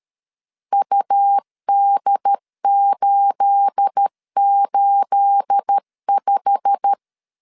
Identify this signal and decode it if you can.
Morse code, heard as a keyed tone, about 13 wpm, 800 Hz. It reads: UD885
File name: CW / DET / 1 9 1500C S A 12wpm 3800